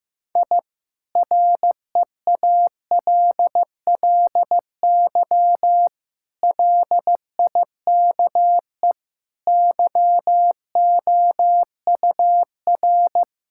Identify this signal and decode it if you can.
Morse code, heard as a keyed tone, about 15 wpm, 700 Hz. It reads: I REALLY LIKE YOUR
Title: Sentences from Top 1000 Words - Review Part 6 - 15wpm